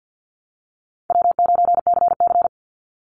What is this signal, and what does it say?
Morse code, heard as a keyed tone, about 50 wpm, 700 Hz. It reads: W9CC
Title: Easy Call Signs - 50wpm